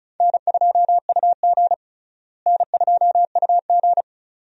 Morse code, heard as a keyed tone, about 36 wpm, 700 Hz. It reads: N2UG N2UG
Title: USA Call Signs - 4 Characters - Repeated 2x - 35wpm